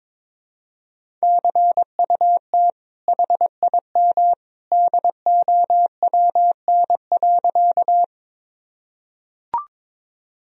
Morse code, heard as a keyed tone, about 22 wpm, 700 Hz. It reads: CUT HIM DOWN.
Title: Sentences from Top 1000 Words - Review Part 1 - 22wpm